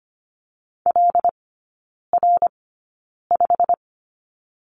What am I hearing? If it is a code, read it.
Morse code, heard as a keyed tone, about 25 wpm, spaced as for 9 wpm, 690 Hz. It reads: LR5
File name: Lesson 15e - Number 5 - Three Characters 25wpm